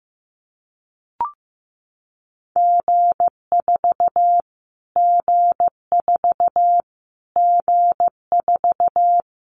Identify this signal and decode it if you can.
Morse code, heard as a keyed tone, about 15 wpm, 700 Hz. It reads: G4 G4 G4
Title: Callsign Prefixes for Top 25 DXCC - Repeated 3x - 15wpm